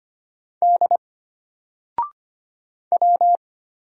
Morse code, heard as a keyed tone, about 25 wpm, 700 Hz. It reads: D W
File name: Single-Letter-Number-25wpm